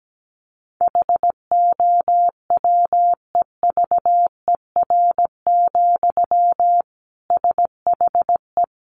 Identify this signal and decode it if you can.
Morse code, heard as a keyed tone, about 17 wpm, 700 Hz. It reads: HOWEVER, SHE
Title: Sentences from Top 600 Words 17wpm